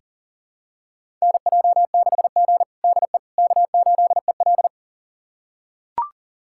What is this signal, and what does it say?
Morse code, heard as a keyed tone, about 37 wpm, 700 Hz. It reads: NJ6G DE K8EL
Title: USA Callsign DE Callsign - 4 Characters - 40wpm